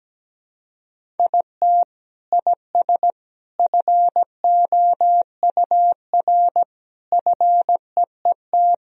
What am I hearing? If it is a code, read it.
Morse code, heard as a keyed tone, about 17 wpm, 700 Hz. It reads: IT IS FOUR FEET
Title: Sentences from Top 1000 Words - Review Part 1 - 17wpm